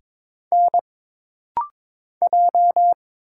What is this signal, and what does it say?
Morse code, heard as a keyed tone, about 22 wpm, 700 Hz. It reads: N J
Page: This rapid-fire timing is geared to push your ICR (Instant Character Recognition) skills.